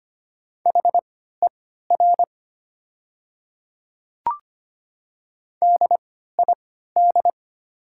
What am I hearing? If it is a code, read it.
Morse code, heard as a keyed tone, about 25 wpm, spaced as for 14 wpm, 700 Hz. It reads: HER DID
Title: 3x Character Spacing - Top 100 Words - 25wpm